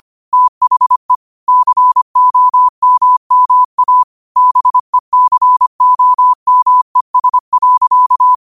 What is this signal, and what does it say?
Morse code, heard as a keyed tone, about 25 wpm, 1000 Hz. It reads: THE COMMA BECOMES.